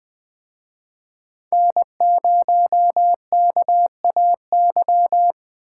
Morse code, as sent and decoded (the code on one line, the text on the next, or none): -. ----- -.- .- -.--
N0KAY